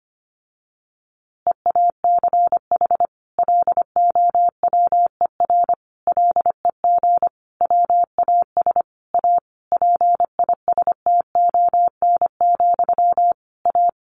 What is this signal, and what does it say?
Morse code, heard as a keyed tone, about 25 wpm, 700 Hz. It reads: EACH LOWER LEG WAS A PISTON, A
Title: A Sound Of Thunder 25wpm